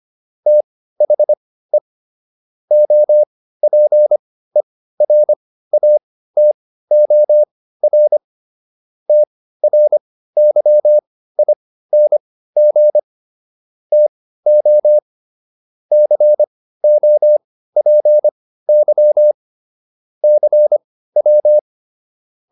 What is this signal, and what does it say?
Morse code, heard as a keyed tone, about 25 wpm, spaced as for 15 wpm, 600 Hz. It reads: THE OPERATOR TRYING TO COPY CW